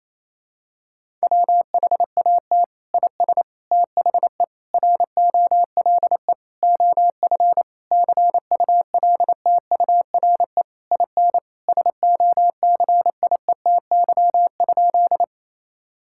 Morse code, heard as a keyed tone, about 28 wpm, 700 Hz. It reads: WHAT IS THE ROLE OF CULTURE IN SOCIETY?
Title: Sentences from Top 1000 Words - Review Part 6 - 28wpm